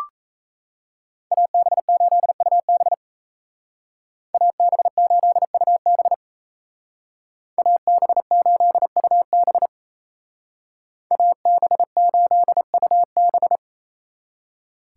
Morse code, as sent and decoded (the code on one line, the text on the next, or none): .- -... ---.. ..- -... / .- -... ---.. ..- -... / .- -... ---.. ..- -... / .- -... ---.. ..- -...
AB8UB AB8UB AB8UB AB8UB